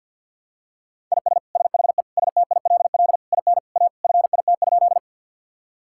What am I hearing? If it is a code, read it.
Morse code, heard as a keyed tone, about 50 wpm, 700 Hz. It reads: IS SHE STILL IN A FIT?
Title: Sentences from Top 1000 Words - Review Part 5 - 50wpm